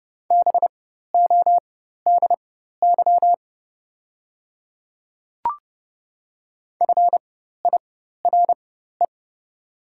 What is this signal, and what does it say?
Morse code, heard as a keyed tone, about 30 wpm, spaced as for 14 wpm, 690 Hz. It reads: BODY FIRE
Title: 4x Character Spacing - Top 500 Words - 30wpm